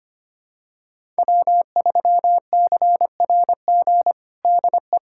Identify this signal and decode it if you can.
Morse code, heard as a keyed tone, about 25 wpm, 700 Hz. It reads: W3CRG DE
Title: USA Callsign DE Callsign - 5 Characters - 25wpm